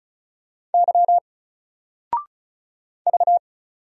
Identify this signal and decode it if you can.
Morse code, heard as a keyed tone, about 36 wpm, 700 Hz. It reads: Y V
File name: Single-Letter-Number-35wpm